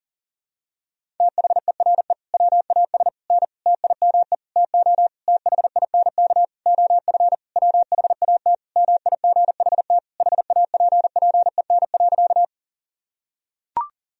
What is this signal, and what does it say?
Morse code, heard as a keyed tone, about 37 wpm, 700 Hz. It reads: THERE WAS N TIME TO THINK OF WHAT MIGHT HAPPEN.
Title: Sentences from Top 400 Words and Abbreviations 40wpm